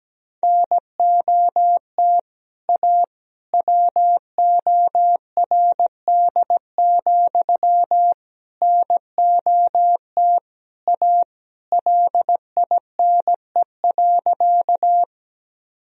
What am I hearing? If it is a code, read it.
Morse code, heard as a keyed tone, about 17 wpm, 700 Hz. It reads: NOT A WORD, NOT A LINE.
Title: Sentences from Top 1000 Words - Review Part 1 - 17wpm